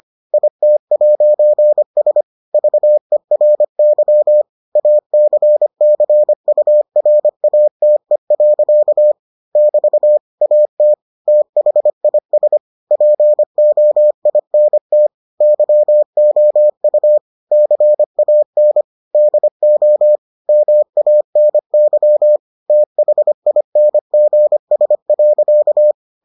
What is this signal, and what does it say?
Morse code, heard as a keyed tone, about 25 wpm, 600 Hz. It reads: IT'S VERY ACCURATE. <BT> AT THIS POINT YOU CAN DO MANY THINGS.